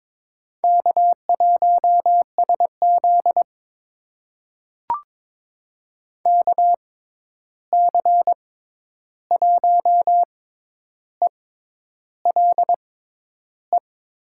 Morse code, heard as a keyed tone, about 22 wpm, spaced as for 10 wpm, 700 Hz. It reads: K1SZ KC1ELE
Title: USA Call Signs - 6x Character Spacing - 22wpm